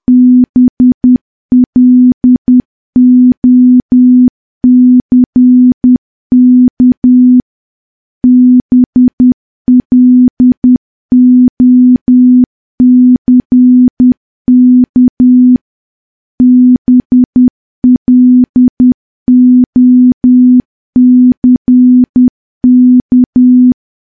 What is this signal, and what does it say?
Morse code, heard as a keyed tone, about 10 wpm, 250 Hz. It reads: BLOCK BLOCK BLOCK
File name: morse / morse-training-10WPM